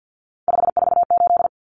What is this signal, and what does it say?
Morse code, heard as a keyed tone, about 50 wpm, 700 Hz. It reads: 548
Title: Numbers - 3-Digit 50wpm